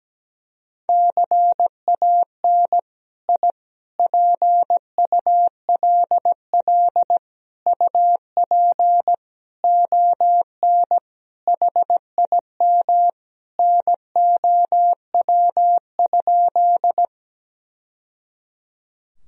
Morse code, heard as a keyed tone, about 17 wpm, 700 Hz. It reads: CAN I PULL UP ON HIM NOW?
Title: Sentences from Top 1000 Words - Review Part 2 - 17wpm